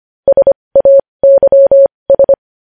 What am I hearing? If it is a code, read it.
Morse code, heard as a keyed tone, about 25 wpm, spaced as for 20 wpm, 550 Hz. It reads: SAYS
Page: news headlines in morse code at 20 wpm